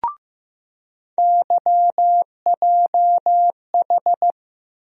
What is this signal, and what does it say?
Morse code, heard as a keyed tone, about 15 wpm, 700 Hz. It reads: YJH
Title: USA Call Sign Postfixes - 3 Letters 15wpm